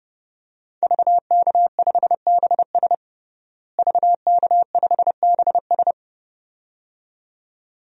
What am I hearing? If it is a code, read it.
Morse code, heard as a keyed tone, about 30 wpm, 700 Hz. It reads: VK5BS VK5BS
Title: DXCC-100 Call Signs - 5 Characters - Repeated 2x - 30wpm